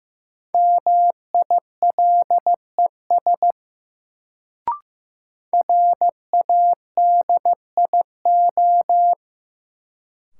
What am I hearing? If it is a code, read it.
Morse code, heard as a keyed tone, about 16 wpm, 700 Hz. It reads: MILES RADIO
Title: Sentences from Top 1000 Words - Review Part 3 - New Word Drill - 15wpm